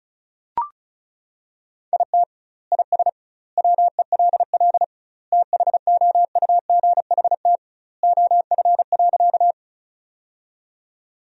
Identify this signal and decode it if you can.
Morse code, heard as a keyed tone, about 36 wpm, 700 Hz. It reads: IT IS WELL THOUGHT OF.
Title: 2x Word-Spacing - Sentences from Top 300 Words - 35wpm